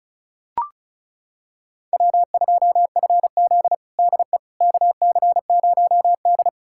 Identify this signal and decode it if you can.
Morse code, heard as a keyed tone, about 34 wpm, 700 Hz. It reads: W2FZ DE KC0D
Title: USA Callsign DE Callsign 35wpm